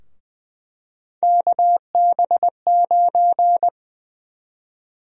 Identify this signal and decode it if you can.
Morse code, heard as a keyed tone, about 20 wpm, 700 Hz. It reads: KB9
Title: USA Call Sign Prefixes - Repeated 2x - 20wpm